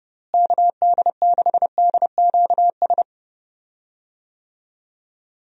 Morse code, heard as a keyed tone, about 30 wpm, 700 Hz. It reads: KD6DQS
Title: USA Call Signs - 6 Characters - Repeated 2x - 30wpm